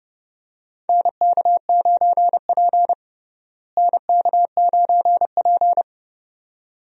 Morse code, heard as a keyed tone, about 30 wpm, 700 Hz. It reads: NK9P NK9P
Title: USA Call Signs - 4 Characters - Repeated 2x - 30wpm